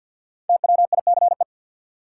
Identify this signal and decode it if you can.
Morse code, heard as a keyed tone, about 50 wpm, 700 Hz. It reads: TWICE
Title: Five Letter Words - 50wpm